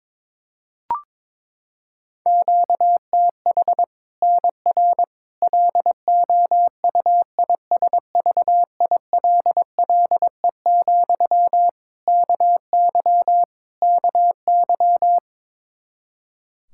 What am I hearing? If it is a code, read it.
Morse code, heard as a keyed tone, about 22 wpm, 700 Hz. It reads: QTH NR LOUISVILLE, KY KY
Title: K1USN SST QSO Practice 22wpm